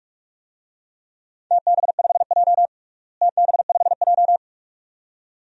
Morse code, heard as a keyed tone, about 47 wpm, 700 Hz. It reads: TB5J TB5J
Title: DXCC-100 Call Signs - 4 Characters - Repeated 2x - 45wpm